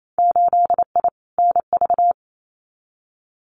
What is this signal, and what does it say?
Morse code, heard as a keyed tone, about 28 wpm, 700 Hz. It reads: 8I NV
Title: Winter Field Day Exchanges Encore 28wpm